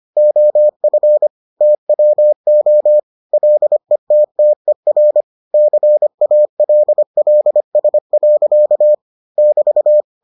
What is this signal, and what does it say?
Morse code, heard as a keyed tone, about 25 wpm, 600 Hz. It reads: OF TWO LETTER CALLS. <BT>